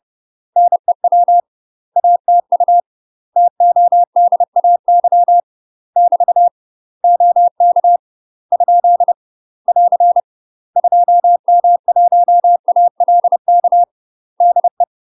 Morse code, heard as a keyed tone, about 30 wpm, 700 Hz. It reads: NEW ATU TODAY <BT> OK ? <AR> 2M1ALK DE
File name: UK MASTER QSOs TRAINING / 30WPM / UK MASTER QSOs 0011